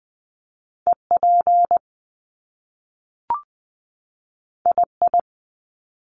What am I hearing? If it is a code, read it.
Morse code, heard as a keyed tone, about 20 wpm, 700 Hz. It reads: E P I I
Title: Top Two Letter Combinations in Words 20wpm